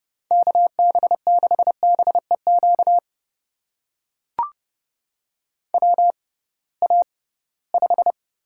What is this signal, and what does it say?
Morse code, heard as a keyed tone, about 30 wpm, spaced as for 20 wpm, 700 Hz. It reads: KB6BEQ WA5
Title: USA Call Signs - 6 Characters - 6x Character Spacing - 30wpm